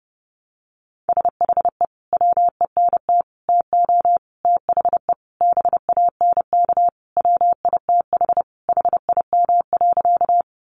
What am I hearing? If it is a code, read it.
Morse code, heard as a keyed tone, about 30 wpm, 700 Hz. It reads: SHE WENT TO THE BANK WITH HIM.
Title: Sentences from Top 900 Words 30wpm